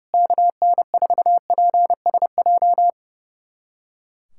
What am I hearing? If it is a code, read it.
Morse code, heard as a keyed tone, about 29 wpm, 700 Hz. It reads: KN4PSJ
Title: USA Callsign DE Callsign 30wpm